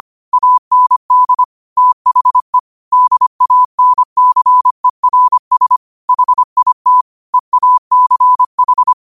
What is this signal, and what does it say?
Morse code, heard as a keyed tone, about 25 wpm, 1000 Hz. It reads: AND THE DANCERS HIT EACH
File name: qotdfiles / 25wpm